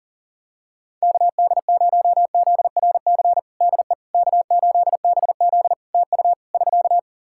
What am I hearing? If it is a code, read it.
Morse code, heard as a keyed tone, about 40 wpm, 700 Hz. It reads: KD0ZRC DE K8BZ TU <SK>
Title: Practice QSO - Exchange 4 - Ending - 40wpm